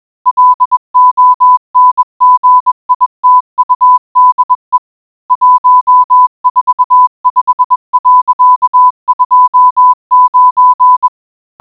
Morse code, heard as a keyed tone, about 21 wpm, 1000 Hz. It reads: LONGITUDE 145.29